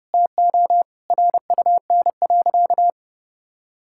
Morse code, heard as a keyed tone, about 32 wpm, 700 Hz. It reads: TO RUN.
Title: Sentences from Top 1600 Words 30wpm